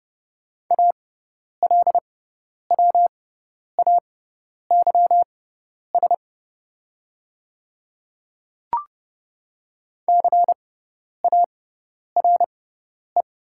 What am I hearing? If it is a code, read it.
Morse code, heard as a keyed tone, about 30 wpm, spaced as for 10 wpm, 700 Hz. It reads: ALWAYS CARE
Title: 6x Character Spacing - Top 300 Words - 30wpm